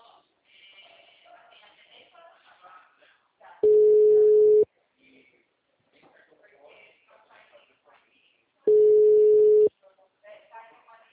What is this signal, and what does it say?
Ringback tone (Europe, 425 Hz), with speech.
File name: out- -2011- - - .17702